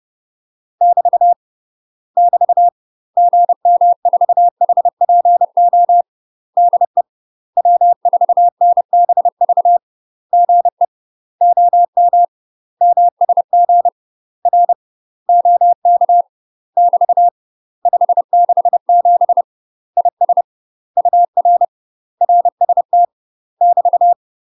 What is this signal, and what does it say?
Morse code, heard as a keyed tone, about 30 wpm, 700 Hz. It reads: <BT> <BT> GM4HPO DE W4NBV GE OM MSG R OK <BT> 567 IS UR RST <BT>
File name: UK MASTER QSOs TRAINING / 30WPM / UK MASTER QSOs 0051